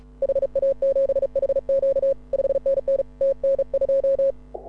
Morse code, heard as a keyed tone, about 36 wpm, 530 Hz. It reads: HA7HQ 5NN TN2
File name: AUDIO2018 / HA6OI CW